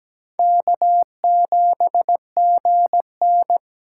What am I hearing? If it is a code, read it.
Morse code, heard as a keyed tone, about 20 wpm, 700 Hz. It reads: K7GN
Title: Speed Racing - USA 1x2 Call Signs 17wpm